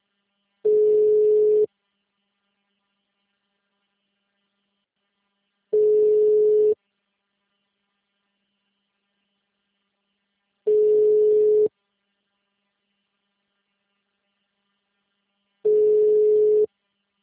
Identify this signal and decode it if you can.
Ringback tone (Europe, 425 Hz).